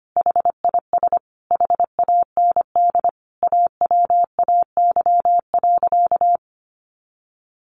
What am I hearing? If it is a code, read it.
Morse code, heard as a keyed tone, about 25 wpm, 700 Hz. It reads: HIS HAND AWAY.
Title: A Sound Of Thunder 25wpm